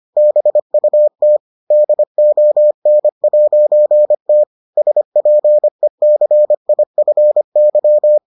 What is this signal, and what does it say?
Morse code, heard as a keyed tone, about 25 wpm, 600 Hz. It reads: BUT DON'T SPECIFY